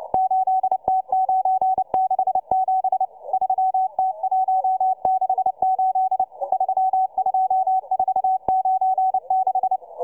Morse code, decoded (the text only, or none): 8T967 3T168 32496